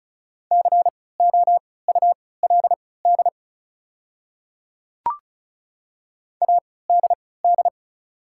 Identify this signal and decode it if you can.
Morse code, heard as a keyed tone, about 36 wpm, spaced as for 20 wpm, 700 Hz. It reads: COULD ADD
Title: 3x Character Spacing - Top 300 Words - 35wpm